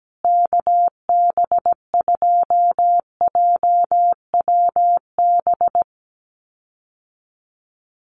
Morse code, heard as a keyed tone, about 17 wpm, 700 Hz. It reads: KB2JWB
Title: USA Call Signs Repeated 3x 17wpm